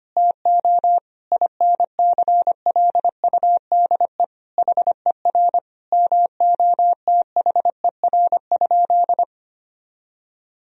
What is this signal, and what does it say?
Morse code, heard as a keyed tone, about 25 wpm, 700 Hz. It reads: TO INCLUDE HER MOTHER?
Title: Sentences from Top 1000 Words - Review Part 3 - 25wpm